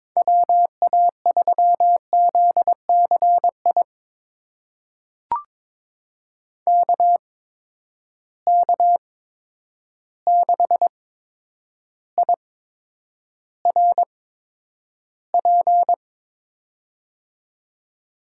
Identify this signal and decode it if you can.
Morse code, heard as a keyed tone, about 22 wpm, spaced as for 9 wpm, 700 Hz. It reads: WA3ZCI KK6IRP
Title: USA Call Signs - 8x Character Spacing - 22wpm